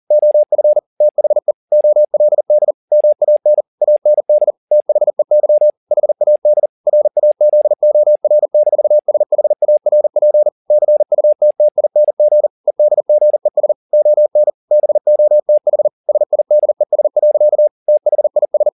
Morse code, heard as a keyed tone, about 40 wpm, 600 Hz. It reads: OF THE OLD MAN AND THEY HAD RAZOR-SHARP CUTTING EDGES ON BOTH SIDES. THIS
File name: the old man and the sea 2 40wpm